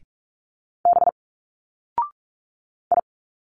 Morse code, heard as a keyed tone, about 43 wpm, 700 Hz. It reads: B I
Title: Single Letters 45wpm